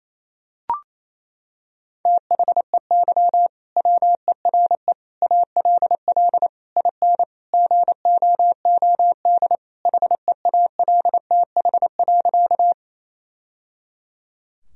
Morse code, heard as a keyed tone, about 28 wpm, 700 Hz. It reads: THEY WERE ALL IN GOOD HEALTH.
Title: Sentences from Top 1000 Words - Review Part 3 - 28wpm